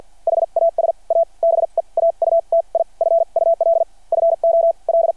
Morse code, heard as a keyed tone, about 52 wpm, spaced as for 39 wpm, 690 Hz. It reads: HAS A BEAUTIFUL FOL
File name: archived / CRSnet